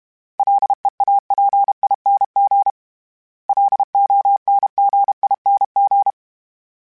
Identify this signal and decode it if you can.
Morse code, heard as a keyed tone, about 32 wpm, 800 Hz. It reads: LEAPING LONGING
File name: Special INGs-mp3s / Special INGs-mp3s / Sngl Ing @30